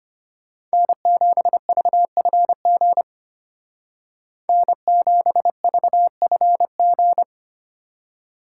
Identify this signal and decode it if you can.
Morse code, heard as a keyed tone, about 31 wpm, 700 Hz. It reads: N7VFG N7VFG
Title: Speed Racing - USA Call Signs 25wpm